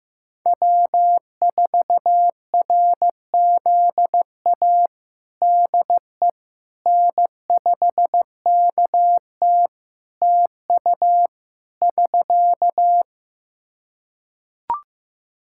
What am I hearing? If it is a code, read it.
Morse code, heard as a keyed tone, about 15 wpm, 700 Hz. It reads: W4RZA DE N5KT TU <SK>
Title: Practice QSO - Exchange 4 - Ending - 15wpm